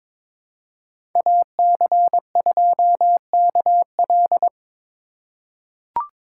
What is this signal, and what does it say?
Morse code, heard as a keyed tone, about 22 wpm, 700 Hz. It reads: AC2KL
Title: USA Call Signs - 5 Characters - 2x Character Spacing - 22wpm